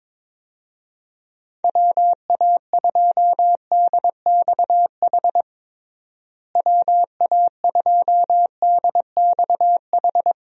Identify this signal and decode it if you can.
Morse code, heard as a keyed tone, about 22 wpm, 700 Hz. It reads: WA2DXH WA2DXH
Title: USA Call Signs - 6 Characters - Repeated 2x - 22wpm